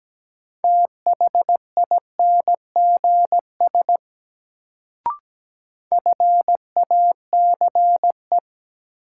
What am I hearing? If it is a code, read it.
Morse code, heard as a keyed tone, about 17 wpm, 700 Hz. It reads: THINGS FACE
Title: Sets of 3 Words - New Word Drill - 17wpm